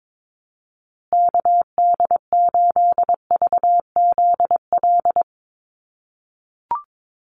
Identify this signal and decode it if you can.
Morse code, heard as a keyed tone, about 22 wpm, 700 Hz. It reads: KD8VZL
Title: USA Call Signs Repeated 3x 22wpm